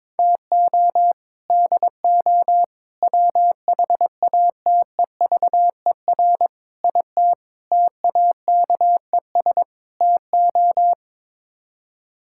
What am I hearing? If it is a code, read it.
Morse code, heard as a keyed tone, about 22 wpm, 700 Hz. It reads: TO DO WHATEVER IT TAKES TO
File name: practice / 22wpm / Sets of 6 Words - 22wpm